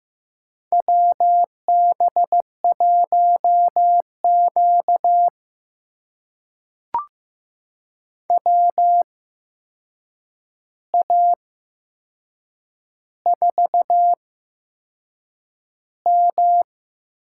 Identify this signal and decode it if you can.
Morse code, heard as a keyed tone, about 15 wpm, spaced as for 6 wpm, 700 Hz. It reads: WB1Q WA4M
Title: USA Call Signs - 8x Character Spacing - 15wpm